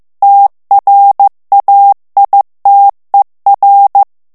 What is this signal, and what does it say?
Morse code, heard as a keyed tone, about 15 wpm, 800 Hz. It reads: TRAITER
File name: morse mp3 / FRANCAIS TEXTE CLAIR 2 VITESSE 14 MOTS MINUTES